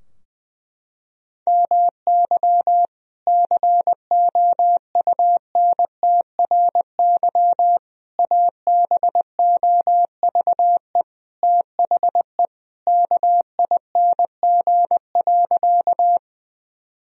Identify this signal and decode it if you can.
Morse code, heard as a keyed tone, about 20 wpm, 700 Hz. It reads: MY COUNTRY ABOVE THE KING.
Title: Sentences from Top 1000 Words - Review Part 2 - 20wpm